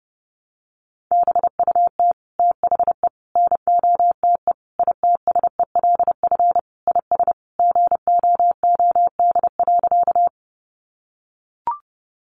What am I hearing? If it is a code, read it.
Morse code, heard as a keyed tone, about 30 wpm, 700 Hz. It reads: BUT THE NOTE ITSELF IS GOOD.
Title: Sentences from Top 800 Words 30wpm